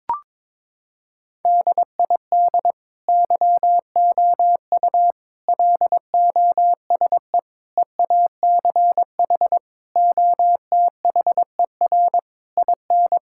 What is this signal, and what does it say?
Morse code, heard as a keyed tone, about 22 wpm, 700 Hz. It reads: DID YOU LOSE EACH OTHER IN
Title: Sentences from Top 1600 Words 22wpm